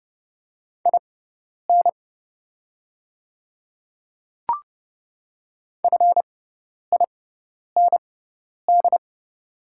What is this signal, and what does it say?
Morse code, heard as a keyed tone, about 30 wpm, spaced as for 10 wpm, 690 Hz. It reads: IN FIND
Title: 6x Character Spacing - Top 100 Words - 30wpm